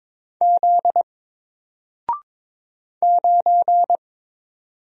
Morse code, heard as a keyed tone, about 22 wpm, 700 Hz. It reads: Z 9
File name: Single-Letter-Number-22wpm